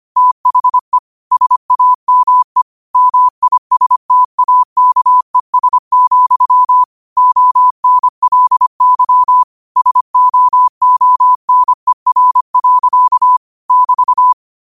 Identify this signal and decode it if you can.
Morse code, heard as a keyed tone, about 25 wpm, 1000 Hz. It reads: THE SAME MISTAKES, ONLY SOONER. <BT>
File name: qotdfiles / 25wpm